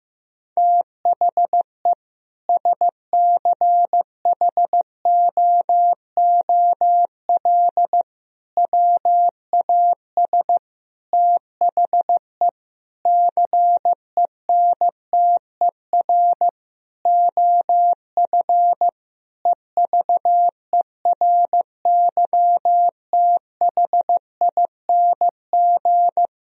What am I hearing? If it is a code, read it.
Morse code, heard as a keyed tone, about 15 wpm, 700 Hz. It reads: THE SCHOOL WAS THE CENTER OF EVERYTHING
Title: Sentences from Top 1000 Words - Review Part 2 - 15wpm